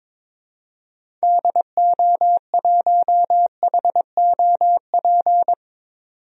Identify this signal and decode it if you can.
Morse code, heard as a keyed tone, about 22 wpm, 700 Hz. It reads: DO1HOP
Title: DXCC-100 Call Signs - 6 Characters - 22wpm